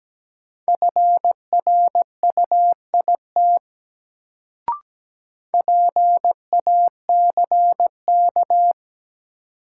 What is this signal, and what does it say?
Morse code, heard as a keyed tone, about 17 wpm, 700 Hz. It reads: FRUIT PACK
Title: Sentences from Top 1300 Words - New Word Drill - 17wpm